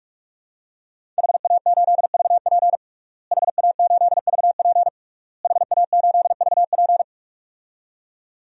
Morse code, heard as a keyed tone, about 46 wpm, 700 Hz. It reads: HA8VP HA8VP HA8VP
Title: DXCC-100 Call Signs - 5 Characters - Repeated 3x - 45wpm